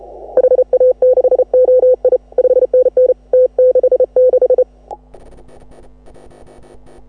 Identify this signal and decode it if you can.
Morse code, heard as a keyed tone, about 33 wpm, 530 Hz. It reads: HA6OI 5NN T66
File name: AUDIO2018 / HA6OI CW